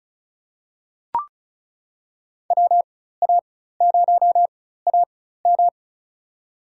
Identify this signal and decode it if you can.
Morse code, heard as a keyed tone, about 36 wpm, spaced as for 16 wpm, 700 Hz. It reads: WA0AM
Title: USA Call Signs - 4x Character Spacing - 35wpm